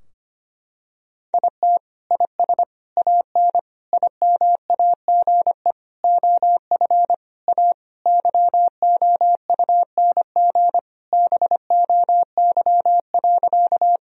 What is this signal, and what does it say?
Morse code, heard as a keyed tone, about 25 wpm, 700 Hz. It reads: IT IS AN IMAGE OF A YOUNG BOY.
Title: Sentences from Top 1000 Words - Review Part 4 - 25wpm